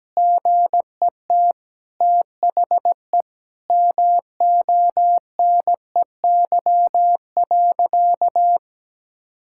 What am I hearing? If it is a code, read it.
Morse code, heard as a keyed tone, about 17 wpm, 700 Hz. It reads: GET THE MONEY.
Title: Sentences from Top 1000 Words - Review Part 5 - 17wpm